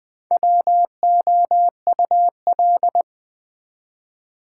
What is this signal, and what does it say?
Morse code, heard as a keyed tone, about 20 wpm, 700 Hz. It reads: WOUL
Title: Build A Word - 1 Letter At A Time - 5-Letter Words - 20wpm